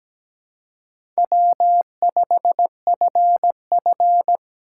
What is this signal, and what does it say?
Morse code, heard as a keyed tone, about 20 wpm, 700 Hz. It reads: W5FF
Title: Speed Racing - USA 1x2 Call Signs 17wpm